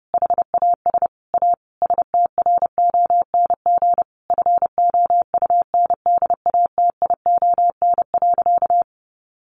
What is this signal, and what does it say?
Morse code, heard as a keyed tone, about 30 wpm, 700 Hz. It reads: HAS A STRONG FOUNDATION.
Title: Sentences from Top 1200 Words 30wpm